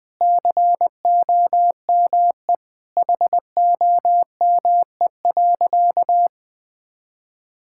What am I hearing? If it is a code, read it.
Morse code, heard as a keyed tone, about 20 wpm, 700 Hz. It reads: COME HOME.
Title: Sentences with 3-Word Phrases - Spoken Beginning - 20wpm